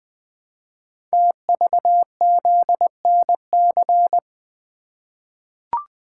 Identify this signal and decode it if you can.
Morse code, heard as a keyed tone, about 20 wpm, 700 Hz. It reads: TVZNC
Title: Code Groups - 5 Letter:Numbers - 20wpm